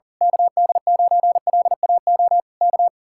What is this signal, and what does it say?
Morse code, heard as a keyed tone, about 38 wpm, 700 Hz. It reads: KD9LAO K
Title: Speed-Racing Common QSO Phrases - 40wpm